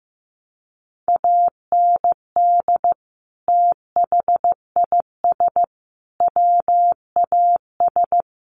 Morse code, heard as a keyed tone, about 14 wpm, 700 Hz. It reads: AND THIS WAS
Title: Sentences from Top 1100 Words 15wpm